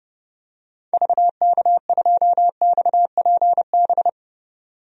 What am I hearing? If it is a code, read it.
Morse code, heard as a keyed tone, about 30 wpm, 700 Hz. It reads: VK2XPB
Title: DXCC-100 Call Signs - 6 Characters - 30wpm